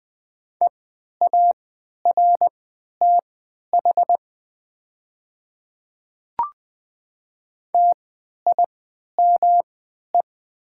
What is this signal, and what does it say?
Morse code, heard as a keyed tone, about 20 wpm, spaced as for 11 wpm, 700 Hz. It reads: EARTH TIME
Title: 3x Character Spacing - Top 200 Words - 20wpm